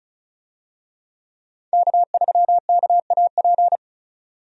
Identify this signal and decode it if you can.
Morse code, heard as a keyed tone, about 36 wpm, 700 Hz. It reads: K3KAP
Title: USA Call Signs - 5 Characters - 35wpm